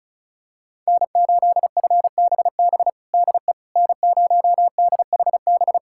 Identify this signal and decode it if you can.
Morse code, heard as a keyed tone, about 36 wpm, 700 Hz. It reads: N8FBB DE N0DHB
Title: USA Callsign DE Callsign - 5 Characters - 35wpm